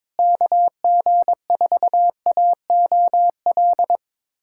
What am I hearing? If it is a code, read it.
Morse code, heard as a keyed tone, about 22 wpm, 700 Hz. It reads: KG4AOL
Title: USA Callsign DE Callsign - 6 Characters - 22wpm